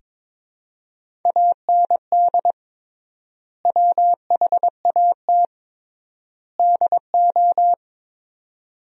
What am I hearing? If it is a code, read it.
Morse code, heard as a keyed tone, about 22 wpm, 700 Hz. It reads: AND WHAT DO
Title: 3x Word-Spacing - Sentences from Top 100 Words - 22wpm